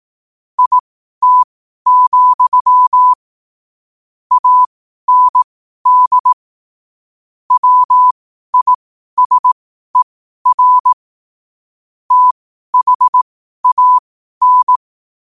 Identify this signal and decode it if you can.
Morse code, heard as a keyed tone, about 18 wpm, spaced as for 12 wpm, 1000 Hz. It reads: IT, AND WISER THAN
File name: qotdfiles / 12wpm